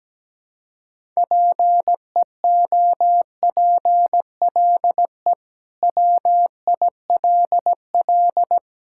Morse code, heard as a keyed tone, about 17 wpm, 700 Hz. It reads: PEOPLE WILL
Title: Sentences from Top 1000 Words - Review Part 6 - 17wpm